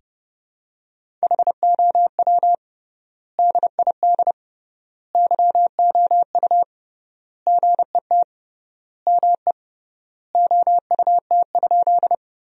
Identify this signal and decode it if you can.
Morse code, heard as a keyed tone, about 30 wpm, 700 Hz. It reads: HOW DID YOU GET ME OUT?
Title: 3x Word-Spacing - Sentences from Top 200 Words - 30wpm